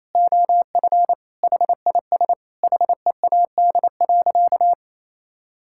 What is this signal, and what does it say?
Morse code, heard as a keyed tone, about 28 wpm, 700 Hz. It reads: OF HIS HEAD.
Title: Sentences from Top 1600 Words 28wpm